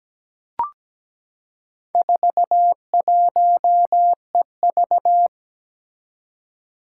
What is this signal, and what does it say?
Morse code, heard as a keyed tone, about 17 wpm, 700 Hz. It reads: 41EV
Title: Four Challenging Letter-Number Combinations 17wpm